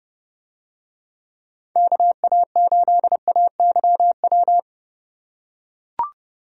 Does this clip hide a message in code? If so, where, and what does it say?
Yes: Morse code from 1.8 to 4.6 s, heard as a keyed tone, about 30 wpm, 700 Hz. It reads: KA8AYW